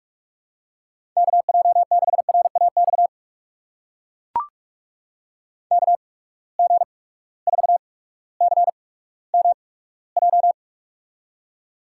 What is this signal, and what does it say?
Morse code, heard as a keyed tone, about 44 wpm, spaced as for 19 wpm, 700 Hz. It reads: KJ6RAX KG4CMJ
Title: USA Call Signs - 6 Characters - 8x Character Spacing - 45wpm